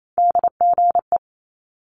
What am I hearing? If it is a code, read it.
Morse code, heard as a keyed tone, about 28 wpm, 700 Hz. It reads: DGE
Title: Multi-letter Phonemes Quick Fire 28wpm